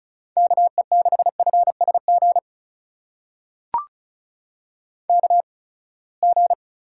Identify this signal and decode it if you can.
Morse code, heard as a keyed tone, about 34 wpm, spaced as for 24 wpm, 700 Hz. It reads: KE6FSG KG
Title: USA Call Signs - 6 Characters - 8x Character Spacing - 35wpm